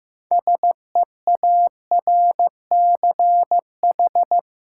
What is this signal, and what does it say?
Morse code, heard as a keyed tone, about 15 wpm, 700 Hz. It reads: SEARCH
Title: Sentences from Top 800 Words - New Word Drill - 15wpm